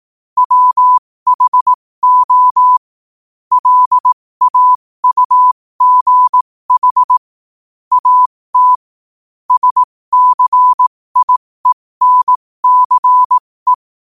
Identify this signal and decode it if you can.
Morse code, heard as a keyed tone, about 18 wpm, spaced as for 15 wpm, 1000 Hz. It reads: WHO LAUGH AT SCIENCE
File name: qotdfiles / 15wpm